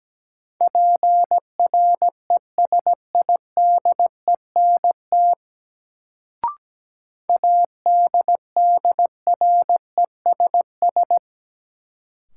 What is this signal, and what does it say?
Morse code, heard as a keyed tone, about 17 wpm, 700 Hz. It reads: PRESIDENT ADDRESS
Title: Sentences from Top 1000 Words - New Word Drill - 17wpm